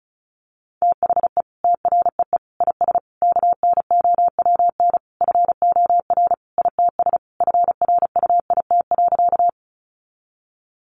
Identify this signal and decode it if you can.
Morse code, heard as a keyed tone, about 34 wpm, 700 Hz. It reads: THE TREE IS KNOWN FOR ITS FRUIT.
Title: Sentences from Top 1300 Words 35wpm